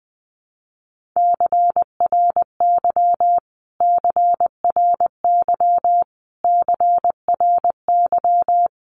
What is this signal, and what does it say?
Morse code, heard as a keyed tone, about 20 wpm, 700 Hz. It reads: CRY CRY CRY
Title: Top 500 Words Repeated 20wpm